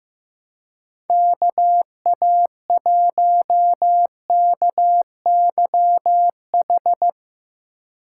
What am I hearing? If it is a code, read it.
Morse code, heard as a keyed tone, about 15 wpm, 700 Hz. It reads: KA1KYH
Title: USA Call Signs - 3x Character Spacing - 15wpm